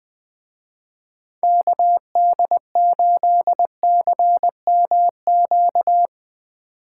Morse code, heard as a keyed tone, about 20 wpm, 700 Hz. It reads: KD8CMQ